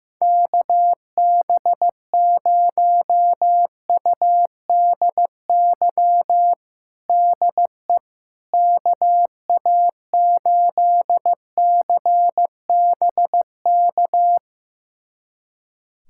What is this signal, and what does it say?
Morse code, heard as a keyed tone, about 15 wpm, 700 Hz. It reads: KB0UDY DE KA8CBK
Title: USA Callsign DE Callsign - 6 Characters - 15wpm